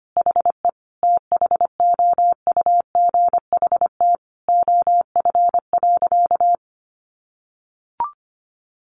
Morse code, heard as a keyed tone, about 25 wpm, 700 Hz. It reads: HE THOUGHT OF.
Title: Sentences from Top 300 Words 25wpm